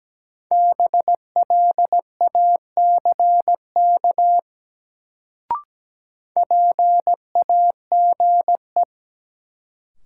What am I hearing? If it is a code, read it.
Morse code, heard as a keyed tone, about 17 wpm, 700 Hz. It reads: BLACK PAGE
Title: Sets of 5 Words Encore - New Word Drill - 17wpm